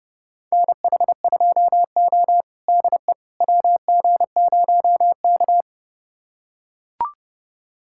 Morse code, heard as a keyed tone, about 31 wpm, 700 Hz. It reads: NH2O DE WG0K
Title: USA Callsign DE Callsign - 4 Characters - 30wpm